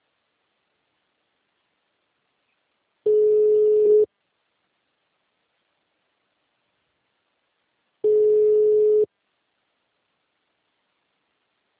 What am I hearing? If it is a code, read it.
Ringback tone (Europe, 425 Hz).
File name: out- -2011- - - .92